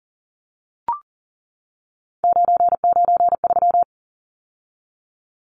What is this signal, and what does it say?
Morse code, heard as a keyed tone, about 40 wpm, 700 Hz. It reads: 993
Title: Numbers - 3-Digit 40wpm